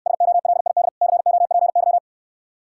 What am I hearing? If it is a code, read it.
Morse code, heard as a keyed tone, about 69 wpm, 700 Hz. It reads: SOLID COPY